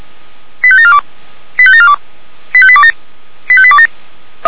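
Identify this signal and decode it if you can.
Five-tone selective call (ZVEI), heard as quick runs of five tones, tones of about 70 ms. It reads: 85742 85742 85827 85827
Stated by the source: Received by SDR; ZVEI five-tone call.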